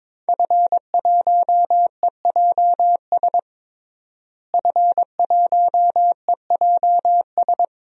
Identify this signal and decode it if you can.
Morse code, heard as a keyed tone, about 22 wpm, 700 Hz. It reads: F1EJS F1EJS
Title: DXCC-100 Call Signs - 5 Characters - Repeated 3x - 22wpm